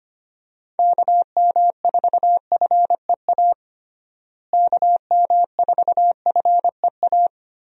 Morse code, heard as a keyed tone, about 25 wpm, 700 Hz. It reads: KM4FEA KM4FEA
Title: USA Call Signs - 6 Characters - Repeated 3x - 25wpm